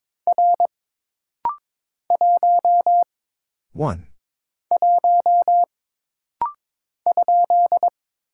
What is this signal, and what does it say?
Morse code, heard as a keyed tone, about 22 wpm, 700 Hz. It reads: R 1 1 ?
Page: This rapid-fire timing is geared to push your ICR (Instant Character Recognition) skills.